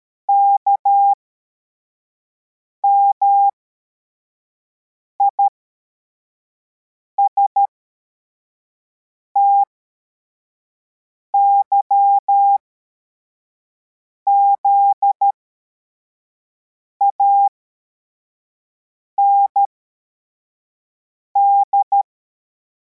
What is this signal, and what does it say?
Morse code, heard as a keyed tone, about 13 wpm, spaced as for 4 wpm, 800 Hz. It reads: KMIST YZAND
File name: formation / cw / Text53